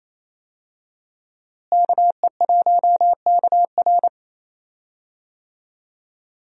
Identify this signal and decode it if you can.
Morse code, heard as a keyed tone, about 28 wpm, 700 Hz. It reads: KE1KR
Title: USA Call Signs - 5 Characters - Repeated 2x - 28wpm